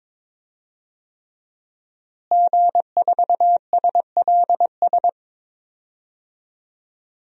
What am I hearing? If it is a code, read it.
Morse code, heard as a keyed tone, about 22 wpm, 700 Hz. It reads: G4SLS
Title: DXCC-100 Call Signs - 5 Characters - Repeated 3x - 22wpm